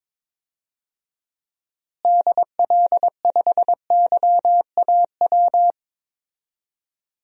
Morse code, heard as a keyed tone, about 22 wpm, 700 Hz. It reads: DL5YAW
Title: DXCC-100 Call Signs - 6 Characters - Repeated 3x - 22wpm